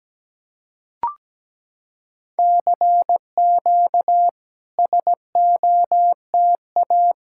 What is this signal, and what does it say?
Morse code, heard as a keyed tone, about 17 wpm, 700 Hz. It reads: CQ SOTA
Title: SOTA QSO Practice 17wpm